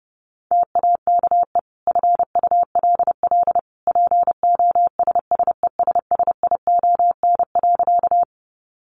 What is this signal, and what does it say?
Morse code, heard as a keyed tone, about 30 wpm, 700 Hz. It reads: TAKE FULL POSSESSION.
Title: Sentences from Top 1400 Words 30wpm